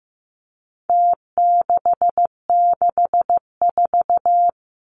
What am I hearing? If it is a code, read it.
Morse code, heard as a keyed tone, about 15 wpm, 700 Hz. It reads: T664
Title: Numbers - Contest Style - 4-Digit 15wpm